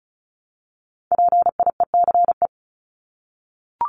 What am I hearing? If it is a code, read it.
Morse code, heard as a keyed tone, about 34 wpm, 700 Hz. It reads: PIECE
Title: Top 500 Words 35wpm